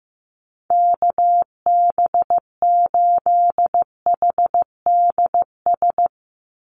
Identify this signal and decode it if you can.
Morse code, heard as a keyed tone, about 15 wpm, 700 Hz. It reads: KB8HDS
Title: USA Call Signs 15wpm